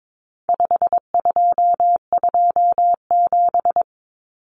Morse code, heard as a keyed tone, about 22 wpm, 700 Hz. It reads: 5227
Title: Common Cut Numbers - 4-Digit 22wpm